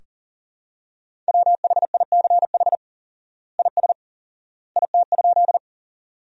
Morse code, heard as a keyed tone, about 37 wpm, 700 Hz. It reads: WHICH IS IT?
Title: 4x Word-Spacing - Sentences from Top 100 Words - 40wpm